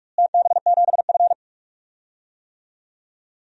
Morse code, heard as a keyed tone, about 46 wpm, 700 Hz. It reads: TB7F
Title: DXCC-100 Call Signs - 4 Characters - Repeated 2x - 45wpm